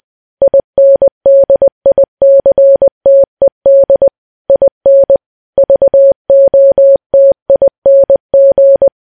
Morse code, heard as a keyed tone, about 20 wpm, 550 Hz. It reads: INDICTED IN VOTING
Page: news-headlines-in-morse-code-at-20-wpm